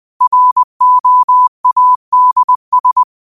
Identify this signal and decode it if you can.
Morse code, heard as a keyed tone, about 21 wpm, 1000 Hz. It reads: ROADS
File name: qotdfiles / 20wpm